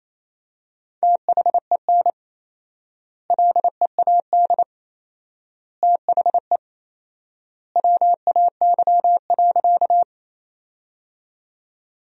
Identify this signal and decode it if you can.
Morse code, heard as a keyed tone, about 28 wpm, 700 Hz. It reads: THEN LEAD THE WAY.
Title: 4x Word-Spacing - Sentences from Top 500 Words - 28wpm